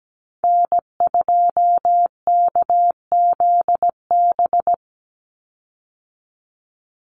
Morse code, heard as a keyed tone, about 17 wpm, 700 Hz. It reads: N2KZB
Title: USA Call Signs 17wpm